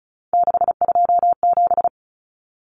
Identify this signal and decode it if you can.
Morse code, heard as a keyed tone, about 34 wpm, 700 Hz. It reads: 627
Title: Numbers - 3-Digit 35wpm